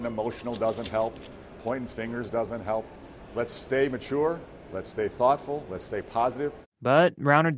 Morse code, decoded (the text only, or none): none